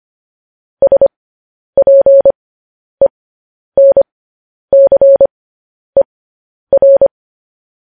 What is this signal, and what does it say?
Morse code, heard as a keyed tone, about 25 wpm, spaced as for 10 wpm, 550 Hz. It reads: SPENCER
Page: News Headlines in Morse Code at 10 WPM